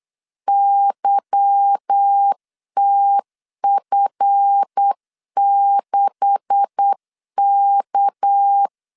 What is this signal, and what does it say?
Morse code, heard as a keyed tone, about 8 wpm, 800 Hz. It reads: YTF6K
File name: LEZIONI / 9 286 8wpm 3425